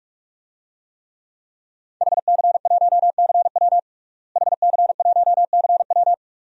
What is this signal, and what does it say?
Morse code, heard as a keyed tone, about 43 wpm, 700 Hz. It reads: HC1CW HC1CW
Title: DXCC-100 Call Signs - 5 Characters - Repeated 3x - 45wpm